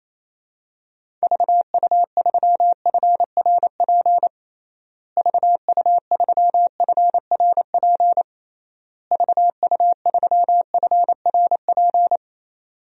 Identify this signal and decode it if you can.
Morse code, heard as a keyed tone, about 28 wpm, 700 Hz. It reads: VU3FRP VU3FRP VU3FRP
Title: DXCC-100 Call Signs - 6 Characters - Repeated 3x - 28wpm